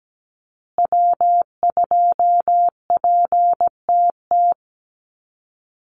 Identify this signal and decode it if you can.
Morse code, heard as a keyed tone, about 17 wpm, 700 Hz. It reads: W2PTT
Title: Call Signs Repeated 2x - 17wpm